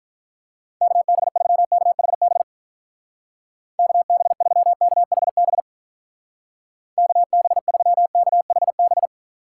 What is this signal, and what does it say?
Morse code, heard as a keyed tone, about 42 wpm, 700 Hz. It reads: KB3KHB KB3KHB KB3KHB
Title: Speed Racing - USA Call Signs 35wpm